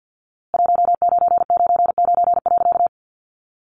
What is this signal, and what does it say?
Morse code, heard as a keyed tone, about 50 wpm, 700 Hz. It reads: 1999.
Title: A Sound Of Thunder 50wpm